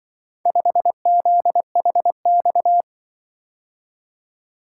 Morse code, heard as a keyed tone, about 24 wpm, 700 Hz. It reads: 5ZHX
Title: Speed-Racing - Four Letter-Number Combinations - 20wpm